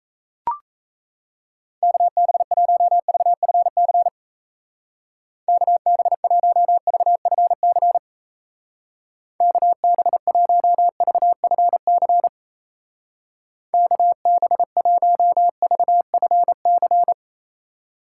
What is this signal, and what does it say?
Morse code, heard as a keyed tone, about 35 wpm, 700 Hz. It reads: KB1VFC KB1VFC KB1VFC KB1VFC